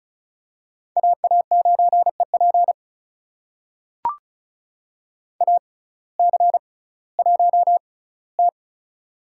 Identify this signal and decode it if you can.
Morse code, heard as a keyed tone, about 34 wpm, spaced as for 19 wpm, 700 Hz. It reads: AA9EP AC1T
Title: USA Call Signs - 6x Character Spacing - 35wpm